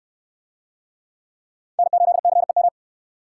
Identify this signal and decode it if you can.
Morse code, heard as a keyed tone, about 66 wpm, 700 Hz. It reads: N8CEM